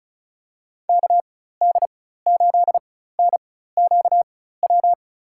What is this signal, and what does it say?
Morse code, heard as a keyed tone, about 34 wpm, spaced as for 16 wpm, 700 Hz. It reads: KD8NQW
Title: USA Call Signs - 6 Characters - 4x Character Spacing - 35wpm